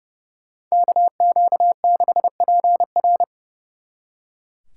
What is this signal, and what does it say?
Morse code, heard as a keyed tone, about 35 wpm, 700 Hz. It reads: KQ6PR
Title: Speed Racing - USA 2x2 Call Signs 30wpm